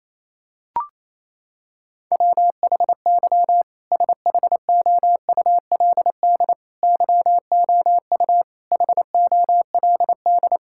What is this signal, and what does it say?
Morse code, heard as a keyed tone, about 27 wpm, 700 Hz. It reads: WHY SHOULD YOU HOLD
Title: Sentences from Top 1600 Words 28wpm